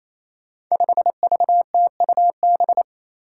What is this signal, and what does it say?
Morse code, heard as a keyed tone, about 29 wpm, 700 Hz. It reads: 5VTUB
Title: Five Challenging Letter-Number Combinations 28wpm